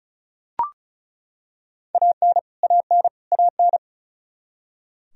Morse code, heard as a keyed tone, about 34 wpm, 700 Hz. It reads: AN AN AN
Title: Top 100 Words Repeated 35wpm